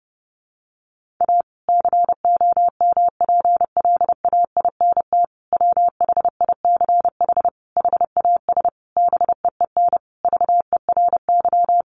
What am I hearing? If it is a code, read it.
Morse code, heard as a keyed tone, about 30 wpm, 700 Hz. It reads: A COMPLAINT WHICH HAS BEEN VERY
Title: Sentences from Top 1400 Words 30wpm